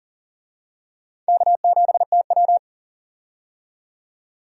Morse code, heard as a keyed tone, about 40 wpm, 700 Hz. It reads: K7TW
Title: USA Call Signs - 4 Characters - 40wpm